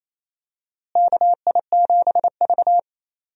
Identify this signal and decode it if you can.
Morse code, heard as a keyed tone, about 27 wpm, 700 Hz. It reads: KI7V
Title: USA Call Signs - 4 Characters - Repeated 2x - 28wpm